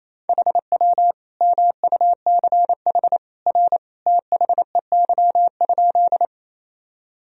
Morse code, heard as a keyed tone, about 27 wpm, 700 Hz. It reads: HW MUCH R THEY?
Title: Sentences from Top 200 Words and Abbreviations 28wpm